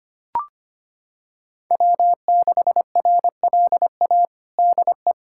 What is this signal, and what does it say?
Morse code, heard as a keyed tone, about 25 wpm, 700 Hz. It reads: W6RLA DE
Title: USA Callsign DE Callsign - 5 Characters - 25wpm